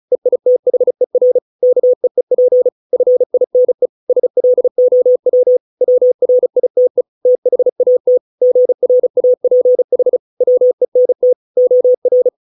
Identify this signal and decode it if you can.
Morse code, heard as a keyed tone, about 34 wpm, 500 Hz. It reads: EITHER KEEP FINE SLOW WRITE THAT GRAPH WENT OR